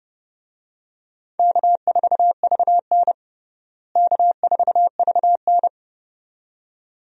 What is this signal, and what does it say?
Morse code, heard as a keyed tone, about 30 wpm, 700 Hz. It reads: K4VN K4VN
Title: USA Call Signs - 4 Characters - Repeated 2x - 30wpm